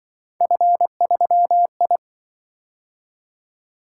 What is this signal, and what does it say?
Morse code, heard as a keyed tone, about 25 wpm, 700 Hz. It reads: F3I